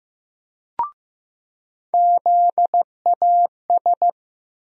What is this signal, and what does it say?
Morse code, heard as a keyed tone, about 15 wpm, 700 Hz. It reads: ZAS
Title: USA Call Sign Postfixes - 3 Letters 15wpm